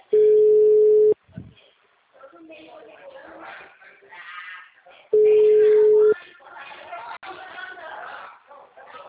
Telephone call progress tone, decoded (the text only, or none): ringback tone